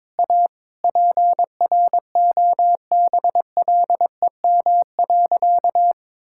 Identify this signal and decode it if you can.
Morse code, heard as a keyed tone, about 22 wpm, 700 Hz. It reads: A PROBLEM.
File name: practice / 22wpm / Sentences with 3-Word Phrases - 22wpm